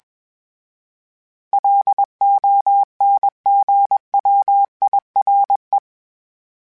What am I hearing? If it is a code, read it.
Morse code, heard as a keyed tone, about 21 wpm, 800 Hz. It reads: LONGWIRE